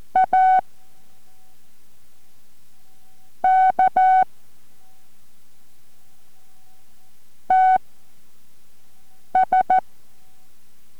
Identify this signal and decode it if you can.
Morse code, heard as a keyed tone, about 14 wpm, spaced as for 5 wpm, 750 Hz. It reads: A K TS
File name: MorseCode-5WPM 4